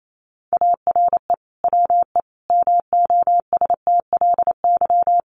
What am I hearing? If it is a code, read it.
Morse code, heard as a keyed tone, about 28 wpm, 700 Hz. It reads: ARE WE MOSTLY
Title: Sentences from Top 1300 Words 28wpm